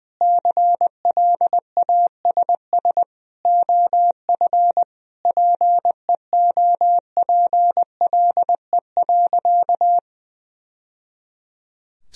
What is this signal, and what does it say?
Morse code, heard as a keyed tone, about 21 wpm, 700 Hz. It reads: CLASS OF PEOPLE.
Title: Sentences from Top 1000 Words - Review Part 2 - 20wpm